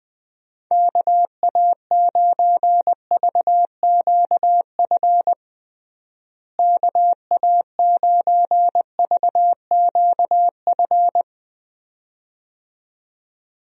Morse code, heard as a keyed tone, about 20 wpm, 700 Hz. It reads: KA9VQF KA9VQF
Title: USA Call Signs - 6 Characters - Repeated 2x - 20wpm